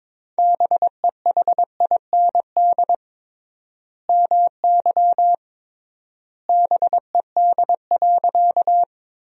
Morse code, heard as a keyed tone, about 22 wpm, 700 Hz. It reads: BEHIND MY BED.
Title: 3x Word-Spacing - Sentences from Top 500 Words - 22wpm